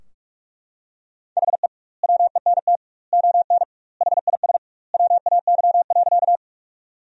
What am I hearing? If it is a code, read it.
Morse code, heard as a keyed tone, about 44 wpm, 700 Hz. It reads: HE WENT ON HIS WAY.
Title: 2x Word-Spacing - Sentences from Top 200 Words - 45wpm